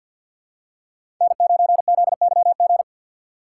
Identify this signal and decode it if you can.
Morse code, heard as a keyed tone, about 52 wpm, 700 Hz. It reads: N9ZYG
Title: USA Call Signs - 5 Characters - 4x Character Spacing - 50wpm